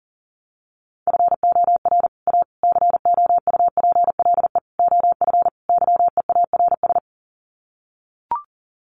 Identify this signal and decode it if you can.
Morse code, heard as a keyed tone, about 40 wpm, 700 Hz. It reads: FOR A COUPLE OF YEARS
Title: Sets of 5 Words Encore 40wpm